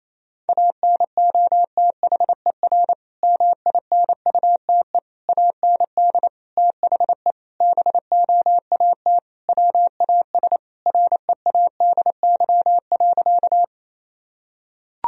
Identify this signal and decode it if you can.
Morse code, heard as a keyed tone, about 28 wpm, 690 Hz. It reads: ANOTHER MINUTE AND THE BOAT WAS READY.
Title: Sentences from Top 1000 Words - Review Part 3 - 28wpm